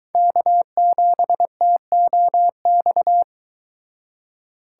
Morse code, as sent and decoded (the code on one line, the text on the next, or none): -.- --... - --- -..-
K7TOX